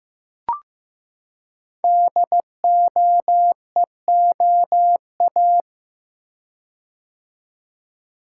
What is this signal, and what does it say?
Morse code, heard as a keyed tone, about 15 wpm, 700 Hz. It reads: DOEOA
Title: Code Groups - 5 Letters - 15wpm